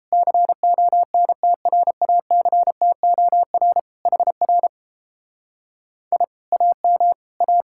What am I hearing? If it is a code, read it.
Morse code, heard as a keyed tone, about 29 wpm, 700 Hz. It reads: CONTRACTOR HR I AM A